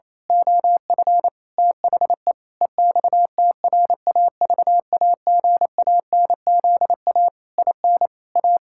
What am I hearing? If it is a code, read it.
Morse code, heard as a keyed tone, about 28 wpm, 700 Hz. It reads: OF THE EXTRAVAGANZA IN A
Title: Pangram Challenge v2 - 28wpm